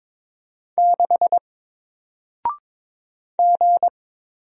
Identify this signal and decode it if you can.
Morse code, heard as a keyed tone, about 22 wpm, 700 Hz. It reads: 6 G
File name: Single-Letter-Number-22wpm